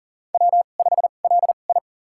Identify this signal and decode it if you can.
Morse code, heard as a keyed tone, about 40 wpm, spaced as for 29 wpm, 700 Hz. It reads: W5LI
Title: USA Call Signs - 4 Characters - 2x Character Spacing - 40wpm